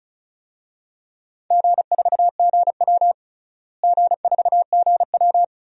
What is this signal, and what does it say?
Morse code, heard as a keyed tone, about 36 wpm, 700 Hz. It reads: G4GW G4GW
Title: DXCC-100 Call Signs - 4 Characters - Repeated 3x - 35wpm